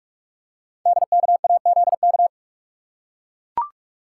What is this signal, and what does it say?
Morse code, heard as a keyed tone, about 44 wpm, 700 Hz. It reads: DKAZK
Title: Code Groups - 5 Letters - 45wpm